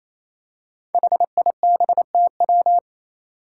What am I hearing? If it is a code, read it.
Morse code, heard as a keyed tone, about 28 wpm, 700 Hz. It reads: HIBTW
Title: Five Challenging Letter-Number Combinations 28wpm